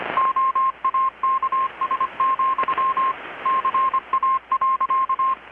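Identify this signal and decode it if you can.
Morse code, heard as a keyed tone, about 25 wpm, 1060 Hz, with radio noise. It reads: OAKS, CA.
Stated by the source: Morse from KPH's Night of Nights XX maritime radiotelegraphy event, received in Chiapas, Mexico about 3,800 km away on a Kenwood R-600 with a grounded 10 m long-wire antenna.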